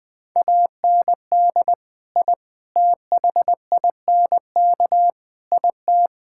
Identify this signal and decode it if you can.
Morse code, heard as a keyed tone, about 20 wpm, 700 Hz. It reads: AND I THINK IT
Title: Build A Phrase - 4-Word Phrases - 20wpm